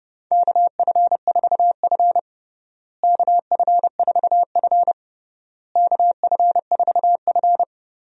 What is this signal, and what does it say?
Morse code, heard as a keyed tone, about 29 wpm, 700 Hz. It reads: KF4F KF4F KF4F
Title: USA Call Signs - 4 Characters - Repeated 3x - 30wpm